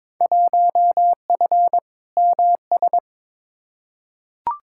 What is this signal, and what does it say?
Morse code, heard as a keyed tone, about 21 wpm, 700 Hz. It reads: 1F MS
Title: Field Day Exchanges Encore - 22wpm